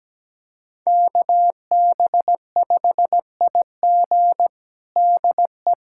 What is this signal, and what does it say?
Morse code, heard as a keyed tone, about 17 wpm, 700 Hz. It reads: KB5IG DE
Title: USA Callsign DE Callsign - 5 Characters - 17wpm